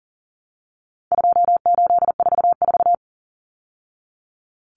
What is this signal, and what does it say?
Morse code, heard as a keyed tone, about 40 wpm, 700 Hz. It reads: 2844
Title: Numbers - 4-Digit 40wpm